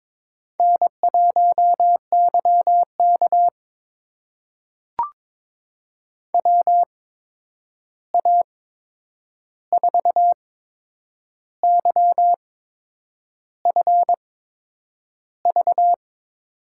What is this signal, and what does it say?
Morse code, heard as a keyed tone, about 22 wpm, spaced as for 8 wpm, 700 Hz. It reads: N1YK WA4YFV
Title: USA Call Signs - 8x Character Spacing - 22wpm